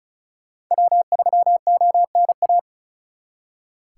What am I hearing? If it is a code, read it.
Morse code, heard as a keyed tone, about 42 wpm, 700 Hz. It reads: W3ONA
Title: Speed Racing - USA 1x3 Call Signs 35wpm